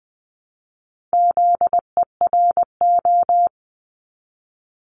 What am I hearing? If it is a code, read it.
Morse code, heard as a keyed tone, about 20 wpm, 700 Hz. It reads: ZERO
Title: Multi-letter Phonemes Repeated 3x 20wpm